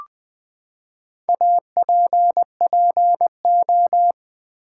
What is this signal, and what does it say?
Morse code, heard as a keyed tone, about 20 wpm, 700 Hz. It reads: APPO
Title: Build A Word - 4 Letters At A Time - 11-Letter Words - 20wpm